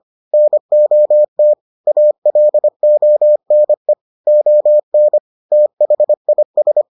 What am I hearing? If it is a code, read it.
Morse code, heard as a keyed tone, about 25 wpm, 600 Hz. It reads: NOT ALONE ON THIS